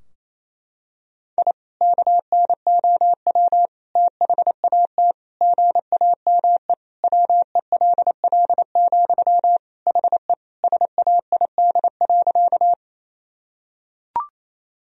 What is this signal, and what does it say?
Morse code, heard as a keyed tone, about 28 wpm, 700 Hz. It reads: I KNOW THAT GAME WELL, HE SAID.
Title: Sentences from Top 1000 Words - Review Part 3 - 28wpm